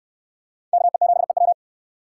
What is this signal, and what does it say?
Morse code, heard as a keyed tone, about 69 wpm, 700 Hz. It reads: KE7EW